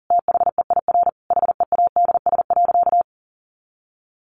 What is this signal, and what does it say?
Morse code, heard as a keyed tone, about 40 wpm, 700 Hz. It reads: THEIR HEADS.
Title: A Sound Of Thunder 40wpm